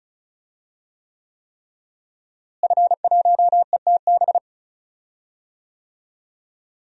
Morse code, heard as a keyed tone, about 36 wpm, 700 Hz. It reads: F1ETB
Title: DXCC-100 Call Signs - 5 Characters - Repeated 3x - 35wpm